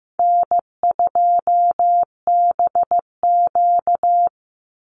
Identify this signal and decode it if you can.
Morse code, heard as a keyed tone, about 15 wpm, 700 Hz. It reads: N2BQ
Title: USA Call Signs 15wpm